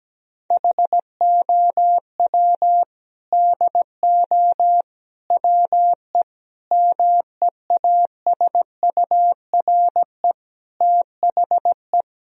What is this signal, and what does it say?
Morse code, heard as a keyed tone, about 17 wpm, 700 Hz. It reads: HOW DO WE MEASURE THE
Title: Sentences from Top 1000 Words - Review Part 3 - 17wpm